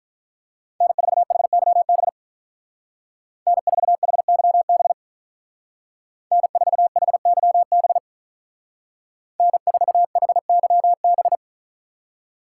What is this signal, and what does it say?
Morse code, heard as a keyed tone, about 42 wpm, 700 Hz. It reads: N4HYB N4HYB N4HYB N4HYB
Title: Speed Racing - USA 1x3 Call Signs 35wpm